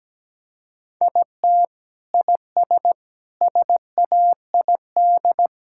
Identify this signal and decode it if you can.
Morse code, heard as a keyed tone, about 17 wpm, 700 Hz. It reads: IT IS SAID
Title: Sentences from Top 1000 Words - Review Part 6 - 17wpm